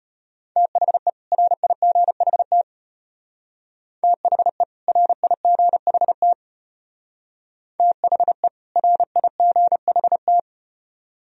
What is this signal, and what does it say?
Morse code, heard as a keyed tone, about 33 wpm, 700 Hz. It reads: THE RIGHT THE RIGHT THE RIGHT